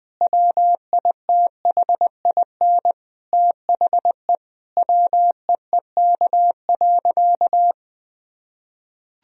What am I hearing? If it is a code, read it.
Morse code, heard as a keyed tone, about 20 wpm, 700 Hz. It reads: WITHIN THE WEEK.
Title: Sentences from Top 1000 Words - Review Part 4 - 20wpm